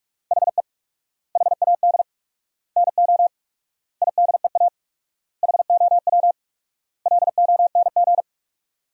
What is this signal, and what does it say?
Morse code, heard as a keyed tone, about 42 wpm, 700 Hz. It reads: HE HAD NO IDEA HOW LONG
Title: 4x Word-Spacing - Sentences from Top 300 Words - 45wpm